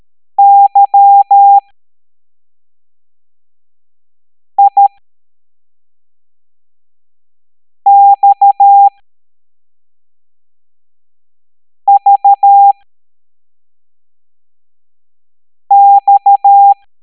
Morse code, heard as a keyed tone, about 13 wpm, spaced as for 3 wpm, 800 Hz. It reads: YIXVX